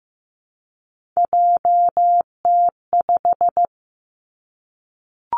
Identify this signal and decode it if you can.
Morse code, heard as a keyed tone, about 15 wpm, 700 Hz. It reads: JT5
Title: Three Challenging Letter-Number Combinations 15wpm